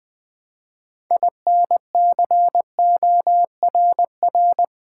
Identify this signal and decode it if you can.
Morse code, heard as a keyed tone, about 20 wpm, 700 Hz. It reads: INCORR
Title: Build A Word - 3 Letters At A Time - 9-Letter Words - 20wpm